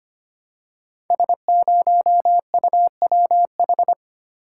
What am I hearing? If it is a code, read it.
Morse code, heard as a keyed tone, about 25 wpm, 700 Hz. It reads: S0UWH
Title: Code Groups - 5 Letter:Numbers - 25wpm